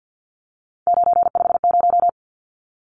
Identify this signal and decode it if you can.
Morse code, heard as a keyed tone, about 50 wpm, 700 Hz. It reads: 950
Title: Numbers - 3-Digit 50wpm